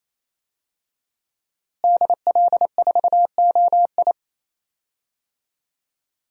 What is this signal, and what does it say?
Morse code, heard as a keyed tone, about 27 wpm, 700 Hz. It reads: DL4OI
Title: USA Call Signs - 5 Characters - Repeated 2x - 28wpm